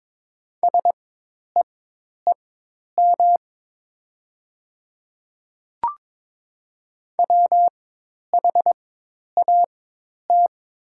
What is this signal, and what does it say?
Morse code, heard as a keyed tone, about 23 wpm, spaced as for 10 wpm, 700 Hz. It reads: SEEM WHAT
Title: 4x Character Spacing - Top 300 Words - 22wpm